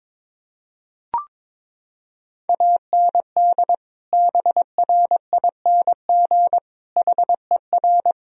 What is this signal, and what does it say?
Morse code, heard as a keyed tone, about 22 wpm, 700 Hz. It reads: AND BRING HER
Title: Sentences from Top 1000 Words - Review Part 3 - 22wpm